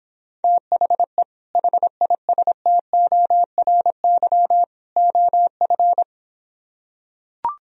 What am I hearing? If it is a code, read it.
Morse code, heard as a keyed tone, about 26 wpm, 700 Hz. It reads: THE HISTORY OF